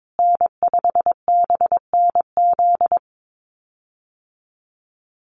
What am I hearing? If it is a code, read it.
Morse code, heard as a keyed tone, about 22 wpm, 700 Hz. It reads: N5BNZ
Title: USA Call Signs Repeated 3x 22wpm